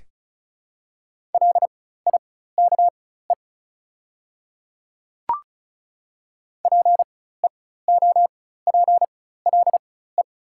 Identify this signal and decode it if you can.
Morse code, heard as a keyed tone, about 34 wpm, spaced as for 16 wpm, 700 Hz. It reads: LIKE PEOPLE
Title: 4x Character Spacing - Top 100 Words - 35wpm